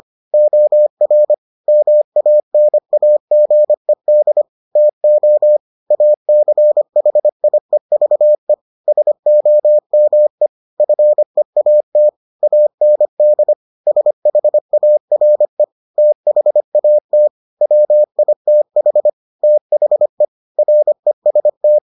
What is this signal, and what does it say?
Morse code, heard as a keyed tone, about 25 wpm, 600 Hz. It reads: OR MANAGED TO ACHIEVE SOME FEAT AND SHARE THAT WITH THE REST